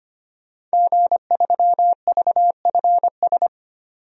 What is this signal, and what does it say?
Morse code, heard as a keyed tone, about 25 wpm, 700 Hz. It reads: G3VFS
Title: DXCC-100 Call Signs - 5 Characters - 25wpm